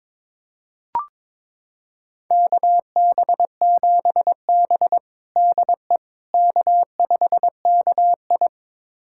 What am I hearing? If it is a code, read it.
Morse code, heard as a keyed tone, about 22 wpm, 700 Hz. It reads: KB7B DE K5KI
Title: USA Callsign DE Callsign - 4 Characters - 22wpm